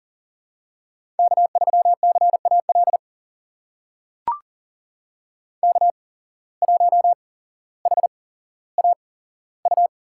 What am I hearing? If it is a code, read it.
Morse code, heard as a keyed tone, about 40 wpm, spaced as for 17 wpm, 700 Hz. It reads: K3CAL K1HAU
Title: USA Call Signs - 5 Characters - 8x Character Spacing - 40wpm